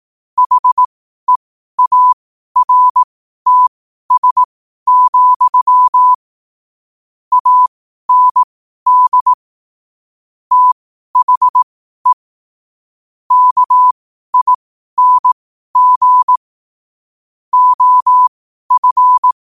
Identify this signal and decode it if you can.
Morse code, heard as a keyed tone, about 18 wpm, spaced as for 12 wpm, 1000 Hz. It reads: HEARTS, AND THE KING OF
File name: qotdfiles / 12wpm